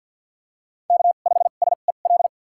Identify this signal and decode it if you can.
Morse code, heard as a keyed tone, about 47 wpm, spaced as for 35 wpm, 700 Hz. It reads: K5SEL
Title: USA Call Signs - 2x Character Spacing - 50wpm